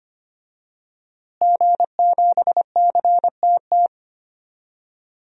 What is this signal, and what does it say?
Morse code, heard as a keyed tone, about 25 wpm, 700 Hz. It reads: G7CTT
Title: DXCC-100 Call Signs - 5 Characters - Repeated 3x - 25wpm